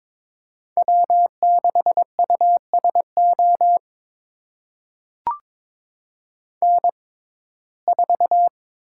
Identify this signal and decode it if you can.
Morse code, heard as a keyed tone, about 22 wpm, spaced as for 16 wpm, 700 Hz. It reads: W6USO N4
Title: USA Call Signs - 6x Character Spacing - 22wpm